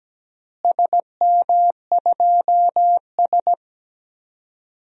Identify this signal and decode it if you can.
Morse code, heard as a keyed tone, about 17 wpm, 700 Hz. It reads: SM2S
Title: DXCC-100 Call Signs - 4 Characters - Repeated 3x - 17wpm